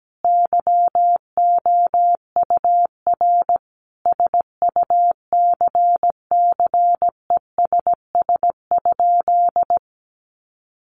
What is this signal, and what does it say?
Morse code, heard as a keyed tone, about 17 wpm, 700 Hz. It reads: YOUR SUCCESS?
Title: Sentences from Top 900 Words 17wpm